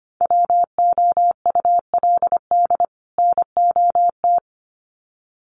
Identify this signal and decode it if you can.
Morse code, heard as a keyed tone, about 25 wpm, 700 Hz. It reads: WOULD NOT
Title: Sets of 2 Words 25wpm